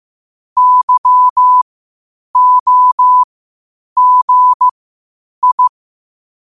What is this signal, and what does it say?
Morse code, heard as a keyed tone, about 15 wpm, spaced as for 8 wpm, 1000 Hz. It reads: YOGI
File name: qotdfiles / 8wpm